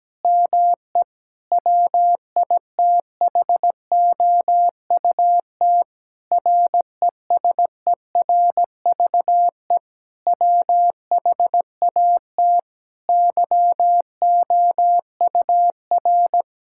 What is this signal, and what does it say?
Morse code, heard as a keyed tone, about 17 wpm, 700 Hz. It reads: ME WITHOUT RESERVE WHAT YOUR
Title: Sentences from Top 1000 Words - Review Part 6 - 17wpm